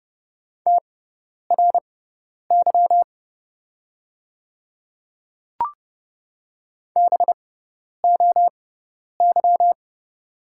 Morse code, heard as a keyed tone, about 30 wpm, spaced as for 10 wpm, 700 Hz. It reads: TRY BOY
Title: 6x Character Spacing - Top 200 Words - 30wpm